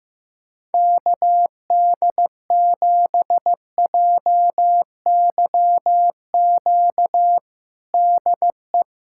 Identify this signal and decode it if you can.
Morse code, heard as a keyed tone, about 15 wpm, 700 Hz. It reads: KD7JYQ DE
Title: USA Callsign DE Callsign 15wpm